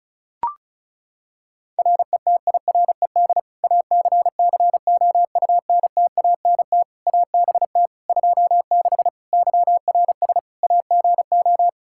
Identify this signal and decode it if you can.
Morse code, heard as a keyed tone, about 34 wpm, 700 Hz. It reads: RETIRED ACCOUNTANT ABT 26 YRS AGO
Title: Practice QSO - Exchange 3 - More Info - 35wpm